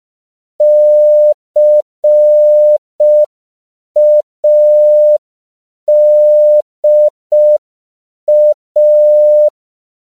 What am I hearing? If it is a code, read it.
Morse code, heard as a keyed tone, about 5 wpm, 600 Hz. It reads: CADA
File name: CW / 5PPM